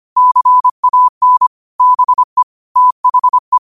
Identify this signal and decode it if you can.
Morse code, heard as a keyed tone, about 25 wpm, 1000 Hz. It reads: CAN BE THE
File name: qotdfiles / 25wpm